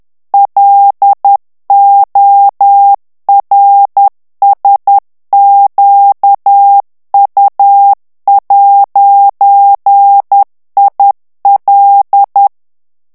Morse code, heard as a keyed tone, about 10 wpm, 800 Hz. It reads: LORSQU'IL
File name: CW / FICHIERS-MP3 / FRANCAIS TEXTE CLAIR 2 VITESSE 10 MOTS MINUTES